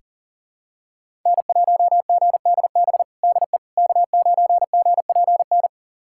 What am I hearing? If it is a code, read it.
Morse code, heard as a keyed tone, about 40 wpm, 700 Hz. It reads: N1GDB DE K9GPN
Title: USA Callsign DE Callsign - 5 Characters - 40wpm